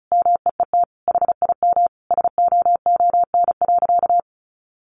Morse code, heard as a keyed tone, about 34 wpm, 700 Hz. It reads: MEET HIM SOON.
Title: Sentences from Top 1100 Words 35wpm